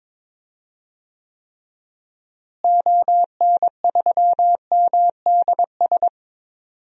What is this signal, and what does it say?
Morse code, heard as a keyed tone, about 22 wpm, 700 Hz. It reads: ON3MDS
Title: DXCC-100 Call Signs - 6 Characters - Repeated 2x - 22wpm